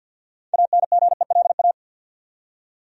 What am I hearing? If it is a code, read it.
Morse code, heard as a keyed tone, about 47 wpm, 700 Hz. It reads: ANGELA
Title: USA Common Names - 50wpm